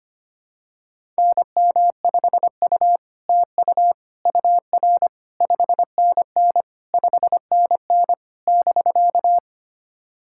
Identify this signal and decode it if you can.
Morse code, heard as a keyed tone, about 25 wpm, 700 Hz. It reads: NM5U TU UR 5NN 5NN <BK>
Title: SOTA QSO Practice 25wpm